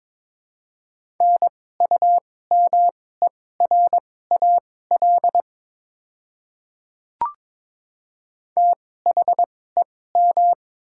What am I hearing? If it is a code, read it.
Morse code, heard as a keyed tone, about 22 wpm, spaced as for 16 wpm, 700 Hz. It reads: NUMERAL THEM
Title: 2x Character Spacing - Top 500 Words - 22wpm